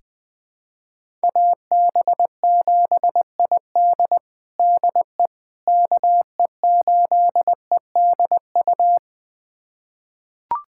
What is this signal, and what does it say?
Morse code, heard as a keyed tone, about 21 wpm, 700 Hz. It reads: AB7ID DE KE8EDU
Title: USA Callsign DE Callsign 20wpm